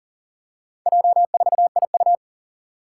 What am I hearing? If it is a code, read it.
Morse code, heard as a keyed tone, about 39 wpm, 700 Hz. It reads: J4IU
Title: Four Challenging Letter-Number Combinations 40wpm